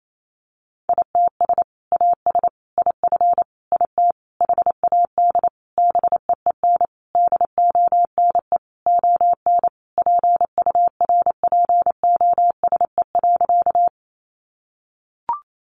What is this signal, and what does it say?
Morse code, heard as a keyed tone, about 28 wpm, 700 Hz. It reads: ITS AS IF IT HAD BEEN DONE ON PURPOSE.
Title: Sentences from Top 800 Words 28wpm